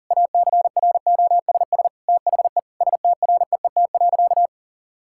Morse code, heard as a keyed tone, about 40 wpm, 700 Hz. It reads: ACROSS THE STREET.
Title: Sentences from Top 1000 Words - Review Part 5 - 40wpm